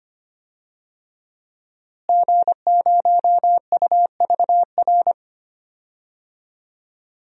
Morse code, heard as a keyed tone, about 25 wpm, 700 Hz. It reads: G0UVR
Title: DXCC-100 Call Signs - 5 Characters - Repeated 3x - 25wpm